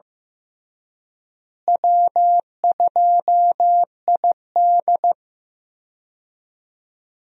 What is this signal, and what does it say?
Morse code, heard as a keyed tone, about 15 wpm, 700 Hz. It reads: W2ID
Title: USA Call Signs - 4 Characters - Repeated 3x - 15wpm